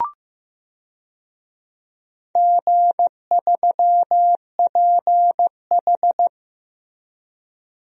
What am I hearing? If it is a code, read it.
Morse code, heard as a keyed tone, about 15 wpm, 700 Hz. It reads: G3PH
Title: DXCC-100 Call Signs - 4 Characters - 15wpm